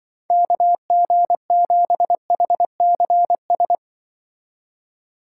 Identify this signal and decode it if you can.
Morse code, heard as a keyed tone, about 24 wpm, 700 Hz. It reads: KG7HCS
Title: Speed Racing - USA 2x3 Call Signs 20wpm